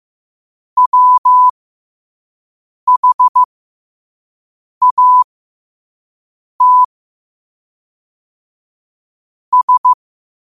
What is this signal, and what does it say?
Morse code, heard as a keyed tone, about 15 wpm, spaced as for 5 wpm, 1000 Hz. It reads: WHAT S
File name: qotdfiles / 5wpm